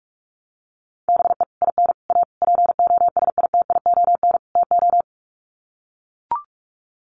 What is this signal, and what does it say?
Morse code, heard as a keyed tone, about 44 wpm, 700 Hz. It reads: BE IN A POSITION TO
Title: Sets of 5 Words Encore2 45wpm